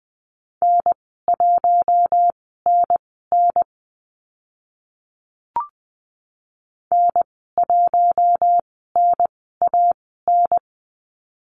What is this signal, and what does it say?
Morse code, heard as a keyed tone, about 20 wpm, spaced as for 14 wpm, 700 Hz. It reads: N1NN N1NAN
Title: 042 Lesson 8g - Number 1 - Call Signs 20wpm